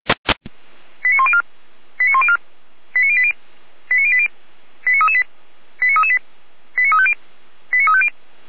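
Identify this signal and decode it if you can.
Five-tone selective call (ZVEI), heard as quick runs of five tones, tones of about 70 ms. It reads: 89104 89104 89080 89080 89338 89338 89350 89350
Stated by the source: ZVEI five-tone call received on SDR.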